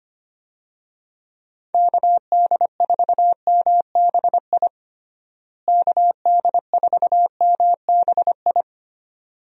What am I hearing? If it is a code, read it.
Morse code, heard as a keyed tone, about 25 wpm, 700 Hz. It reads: KD4MBI KD4MBI
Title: USA Call Signs - 6 Characters - Repeated 2x - 25wpm